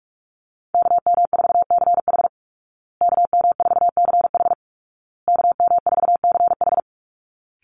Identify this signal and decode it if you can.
Morse code, heard as a keyed tone, about 44 wpm, 700 Hz. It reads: KM4CH KM4CH KM4CH
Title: Call Signs Repeated 2x - 45wpm